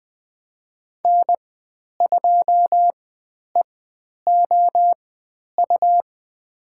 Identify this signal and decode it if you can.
Morse code, heard as a keyed tone, about 20 wpm, spaced as for 10 wpm, 700 Hz. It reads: N2EOU
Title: 102 Lesson 17g - Letter u - Call Signs 20wpm